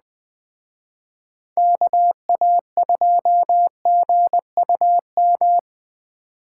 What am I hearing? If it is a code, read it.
Morse code, heard as a keyed tone, about 21 wpm, 700 Hz. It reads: KA2GUM